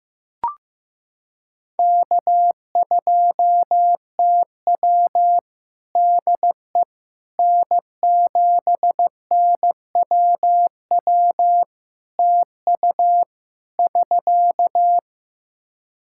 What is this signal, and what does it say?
Morse code, heard as a keyed tone, about 15 wpm, 700 Hz. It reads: K2TW DE N7NWW TU <SK>
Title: K1USN SST QSO Practice 15wpm